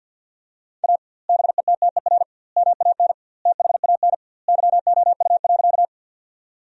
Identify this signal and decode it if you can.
Morse code, heard as a keyed tone, about 50 wpm, 700 Hz. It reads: A BETTER MAN THAN YOU.
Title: 2x Word-Spacing - Sentences from Top 400 Words - 50wpm